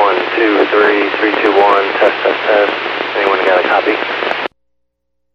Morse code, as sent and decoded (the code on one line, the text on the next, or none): none